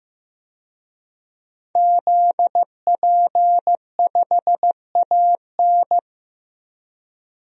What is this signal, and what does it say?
Morse code, heard as a keyed tone, about 16 wpm, 700 Hz. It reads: ZP5AN
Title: DXCC-100 Call Signs - 5 Characters - Repeated 3x - 15wpm